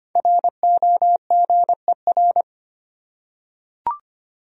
Morse code, heard as a keyed tone, about 25 wpm, 700 Hz. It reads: ROGER
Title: USA Common Names - Max 5 Letters - 25wpm